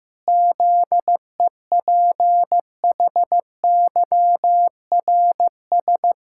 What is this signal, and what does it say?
Morse code, heard as a keyed tone, about 15 wpm, 700 Hz. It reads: ZEPHYRS
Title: Pangram Challenge v2 - 15wpm